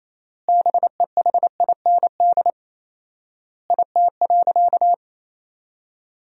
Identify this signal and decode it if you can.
Morse code, heard as a keyed tone, about 28 wpm, 700 Hz. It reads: BEHIND IT.
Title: 4x Word-Spacing - Sentences from Top 500 Words - 28wpm